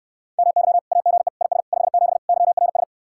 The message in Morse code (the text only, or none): MY AGE IS 57 YRS